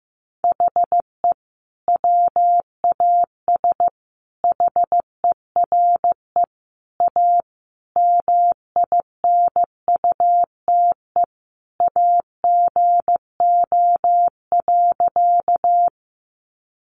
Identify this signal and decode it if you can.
Morse code, heard as a keyed tone, about 15 wpm, 700 Hz. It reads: HE WAS HERE A MINUTE AGO.
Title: Sentences from Top 500 Words 15wpm